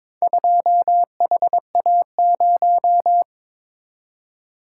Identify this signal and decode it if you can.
Morse code, heard as a keyed tone, about 22 wpm, 700 Hz. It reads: 2HA0
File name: practice / 22wpm / Four Letter-Number Combinations - 22wpm